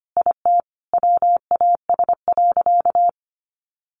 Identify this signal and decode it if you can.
Morse code, heard as a keyed tone, about 25 wpm, 700 Hz. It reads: IT WAS.
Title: Sentences from Top 1200 Words 25wpm